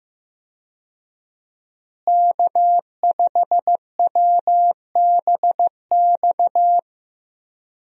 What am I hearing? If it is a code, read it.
Morse code, heard as a keyed tone, about 16 wpm, 700 Hz. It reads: K5WBX
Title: USA Call Signs - 5 Characters - Repeated 2x - 15wpm